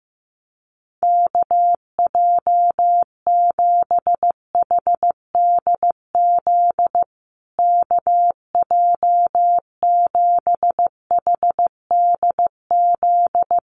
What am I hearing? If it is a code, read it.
Morse code, heard as a keyed tone, about 15 wpm, 700 Hz. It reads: KJ7HDZ KJ7HDZ
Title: USA Call Signs Repeated 3x 15wpm